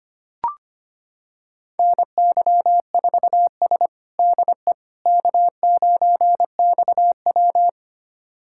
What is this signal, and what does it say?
Morse code, heard as a keyed tone, about 25 wpm, 700 Hz. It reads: NY4S DE K9XW
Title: USA Callsign DE Callsign - 4 Characters - 25wpm